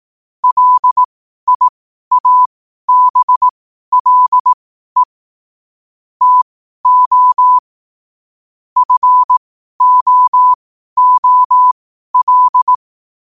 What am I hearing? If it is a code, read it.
Morse code, heard as a keyed tone, about 18 wpm, spaced as for 12 wpm, 1000 Hz. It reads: LIABLE TO FOOL
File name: qotdfiles / 12wpm